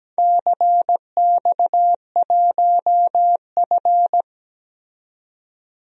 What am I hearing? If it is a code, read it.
Morse code, heard as a keyed tone, about 17 wpm, 700 Hz. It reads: CX1F
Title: DXCC-100 Call Signs - 4 Characters - 17wpm